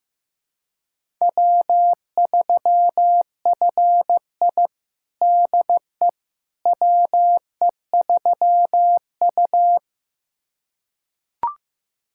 Morse code, heard as a keyed tone, about 16 wpm, 700 Hz. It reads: W3FI DE WE3U
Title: USA Callsign DE Callsign - 4 Characters - 15wpm